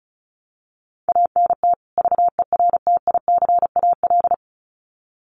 Morse code, heard as a keyed tone, about 34 wpm, 700 Hz. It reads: ANT VERTICAL
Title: Common QSO Phrases 35wpm